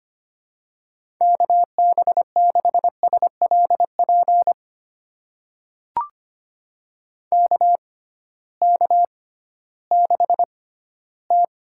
Morse code, heard as a keyed tone, about 25 wpm, spaced as for 14 wpm, 700 Hz. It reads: KB6SLP KK6T
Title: USA Call Signs - 6x Character Spacing - 25wpm